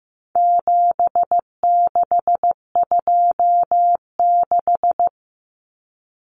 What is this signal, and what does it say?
Morse code, heard as a keyed tone, about 15 wpm, 700 Hz. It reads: 7626
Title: Numbers - 4-Digit 15wpm